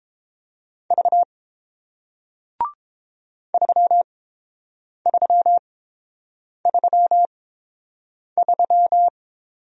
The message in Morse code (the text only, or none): V 3333